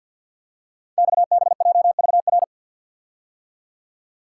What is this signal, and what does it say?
Morse code, heard as a keyed tone, about 48 wpm, 700 Hz. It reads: XBJVR
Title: Code Groups - 5 Letters - 50wpm